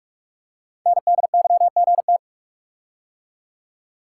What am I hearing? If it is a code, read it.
Morse code, heard as a keyed tone, about 44 wpm, 700 Hz. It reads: NDYGT
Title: Code Groups - 5 Letter:Numbers - 45wpm